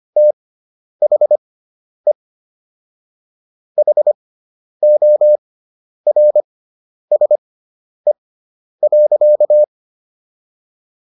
Morse code, heard as a keyed tone, about 25 wpm, spaced as for 10 wpm, 600 Hz. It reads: THE HORSE.